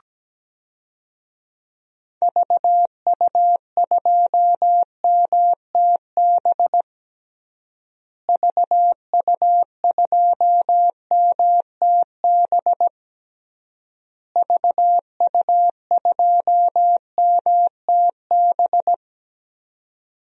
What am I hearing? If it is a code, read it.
Morse code, heard as a keyed tone, about 17 wpm, 700 Hz. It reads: VU2MTB VU2MTB VU2MTB
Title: DXCC-100 Call Signs - 6 Characters - Repeated 3x - 17wpm